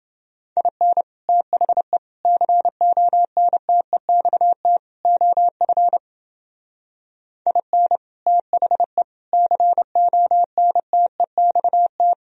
Morse code, heard as a keyed tone, about 29 wpm, 700 Hz. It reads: IN THE CONTEXT OF IN THE CONTEXT